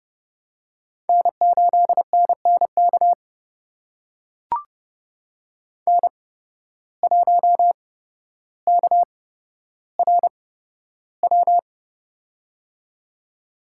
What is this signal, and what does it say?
Morse code, heard as a keyed tone, about 32 wpm, spaced as for 13 wpm, 700 Hz. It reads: N8NNK N1KRW
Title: USA Call Signs - 5 Characters - 8x Character Spacing - 30wpm